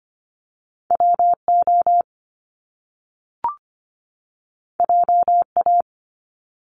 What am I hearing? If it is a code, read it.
Morse code, heard as a keyed tone, about 25 wpm, 700 Hz. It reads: WO JA
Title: Top Two Letter Combinations in Words 25wpm